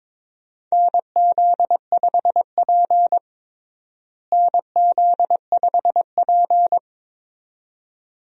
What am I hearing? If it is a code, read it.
Morse code, heard as a keyed tone, about 21 wpm, 700 Hz. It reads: NZ5P NZ5P
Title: USA Call Signs - 4 Characters - Repeated 2x - 22wpm